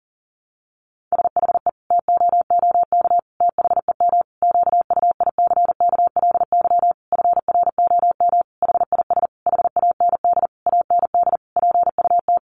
Morse code, heard as a keyed tone, about 40 wpm, 700 Hz. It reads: SHE TOOK THEM QUICKLY FROM HIS HAND AND PUT
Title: Sentences from Top 1300 Words 40wpm